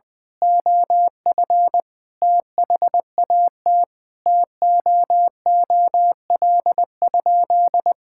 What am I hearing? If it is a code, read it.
Morse code, heard as a keyed tone, about 21 wpm, 700 Hz. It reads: OF THAT TOOL?
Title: Sentences from Top 1000 Words - Review Part 6 - 20wpm